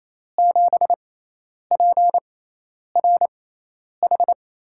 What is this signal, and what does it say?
Morse code, heard as a keyed tone, about 28 wpm, spaced as for 10 wpm, 700 Hz. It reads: 7PRH
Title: USA Call Signs - 6x Character Spacing - 28wpm